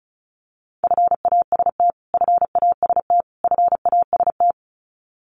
Morse code, heard as a keyed tone, about 34 wpm, 700 Hz. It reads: FAST FAST FAST
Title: Top 500 Words Repeated 35wpm